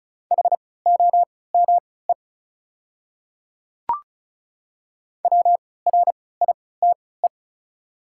Morse code, heard as a keyed tone, about 36 wpm, spaced as for 20 wpm, 690 Hz. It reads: HOME WRITE
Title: 3x Character Spacing - Top 200 Words - 35wpm